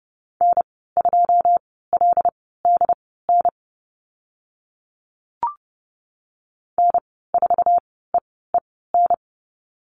Morse code, heard as a keyed tone, about 30 wpm, spaced as for 17 wpm, 700 Hz. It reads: N2LDN N4EEN
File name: Lesson 15g - Number 5 - Call Signs 30wpm